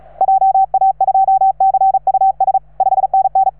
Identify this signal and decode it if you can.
Morse code, heard as a keyed tone, about 34 wpm, 780 Hz. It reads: JA2CUS 5NN